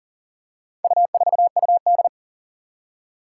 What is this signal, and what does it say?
Morse code, heard as a keyed tone, about 40 wpm, 700 Hz. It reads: U4UD
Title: Four Challenging Letter-Number Combinations 40wpm